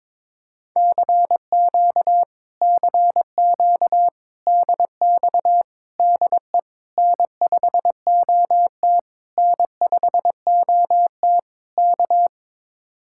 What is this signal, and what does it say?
Morse code, heard as a keyed tone, about 22 wpm, 700 Hz. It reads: CQ CQ DX DE N5OT N5OT K
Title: Speed-Racing Common QSO Phrases Encore - 22wpm